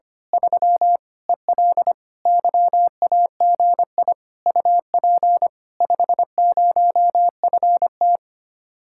Morse code, heard as a keyed tone, about 24 wpm, 700 Hz. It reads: 3 EL YAGI UP 50FT
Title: Speed-Racing Common QSO Phrases - 25wpm